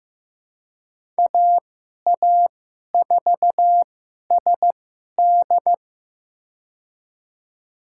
Morse code, heard as a keyed tone, about 15 wpm, spaced as for 11 wpm, 700 Hz. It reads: AA4SD
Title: USA Call Signs - 5 Characters - 2x Character Spacing - 15wpm